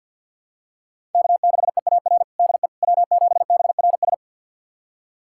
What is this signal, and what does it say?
Morse code, heard as a keyed tone, about 49 wpm, 700 Hz. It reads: K6EAR DE W7BRS
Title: USA Callsign DE Callsign - 5 Characters - 50wpm